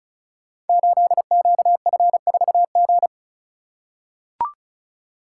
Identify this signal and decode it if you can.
Morse code, heard as a keyed tone, about 36 wpm, 700 Hz. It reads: 8QF4G
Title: Code Groups - 5 Letter:Numbers - 35wpm